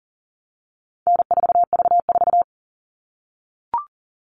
Morse code, heard as a keyed tone, about 40 wpm, 700 Hz. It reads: N4V4
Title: All Cut Numbers - 4-Digit 40wpm